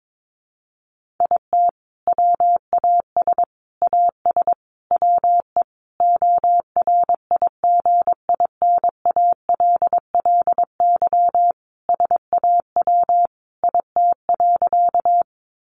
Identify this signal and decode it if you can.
Morse code, heard as a keyed tone, about 22 wpm, 700 Hz. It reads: IT WAS AS WE ORIGINALLY SAW IT.
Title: Sentences from Top 1500 Words 22wpm